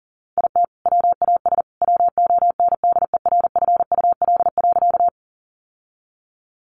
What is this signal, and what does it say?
Morse code, heard as a keyed tone, about 40 wpm, 700 Hz. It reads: IT WAS WONDERFUL.
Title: Sentences from Top 1100 Words 40wpm